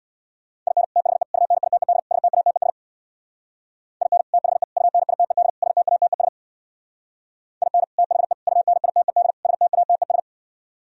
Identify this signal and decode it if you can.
Morse code, heard as a keyed tone, about 64 wpm, 700 Hz. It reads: IN THE UNITED STATES IN THE UNITED STATES IN THE UNITED STATES